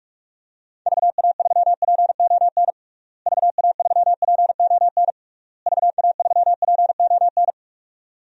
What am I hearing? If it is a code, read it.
Morse code, heard as a keyed tone, about 41 wpm, 700 Hz. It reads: VA3PON VA3PON VA3PON
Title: DXCC-100 Call Signs - 6 Characters - Repeated 3x - 45wpm